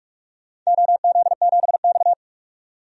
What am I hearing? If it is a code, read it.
Morse code, heard as a keyed tone, about 41 wpm, 700 Hz. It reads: OZ7X
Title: DXCC-100 Call Signs - 4 Characters - 45wpm